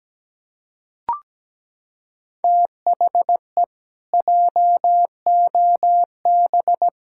Morse code, heard as a keyed tone, about 17 wpm, 700 Hz. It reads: THE JOB
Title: Pangram Challenge v2 - 17wpm